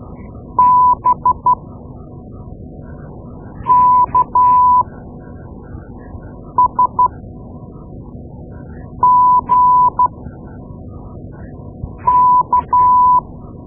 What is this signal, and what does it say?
Morse code, heard as a keyed tone, about 12 wpm, spaced as for 4 wpm, 1000 Hz. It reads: BKSGK
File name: Morse Lessons / Morse Lessons / Tape-2A